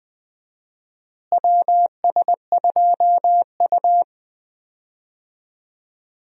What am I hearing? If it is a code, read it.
Morse code, heard as a keyed tone, about 20 wpm, 700 Hz. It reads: WS2U
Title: USA Call Signs - 4 Characters - Repeated 2x - 20wpm